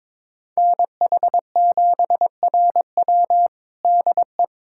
Morse code, heard as a keyed tone, about 22 wpm, 700 Hz. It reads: NH7RW DE
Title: USA Callsign DE Callsign 22wpm